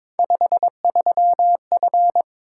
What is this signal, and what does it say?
Morse code, heard as a keyed tone, about 23 wpm, 700 Hz. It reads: 53F
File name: practice / 22wpm / Three Letter-Number Combinations - 22wpm